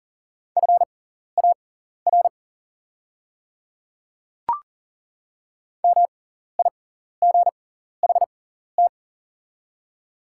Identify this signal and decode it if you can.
Morse code, heard as a keyed tone, about 40 wpm, spaced as for 14 wpm, 690 Hz. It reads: FAR MIGHT
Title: 6x Character Spacing - Top 300 Words - 40wpm